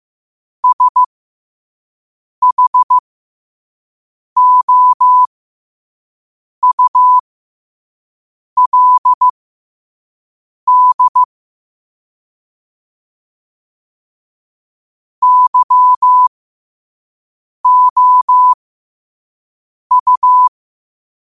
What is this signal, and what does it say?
Morse code, heard as a keyed tone, about 15 wpm, spaced as for 5 wpm, 1000 Hz. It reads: SHOULD YOU
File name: qotdfiles / 5wpm